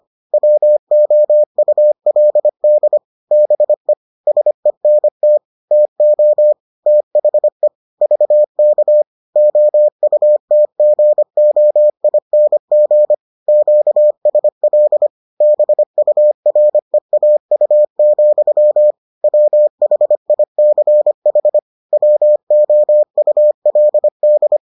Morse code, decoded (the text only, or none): WOULD BE SENT TO THE VK OUTGOING QSL BUREAU, WHICH WOULD